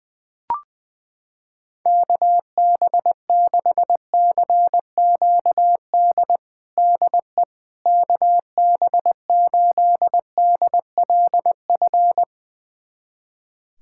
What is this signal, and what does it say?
Morse code, heard as a keyed tone, about 19 wpm, 700 Hz. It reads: KB6CQD DE KB8DLF
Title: USA Callsign DE Callsign 20wpm